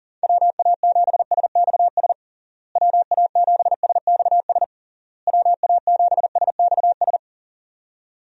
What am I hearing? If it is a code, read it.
Morse code, heard as a keyed tone, about 40 wpm, 700 Hz. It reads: WA7SXS WA7SXS WA7SXS
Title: USA Call Signs - 6 Characters - Repeated 3x - 40wpm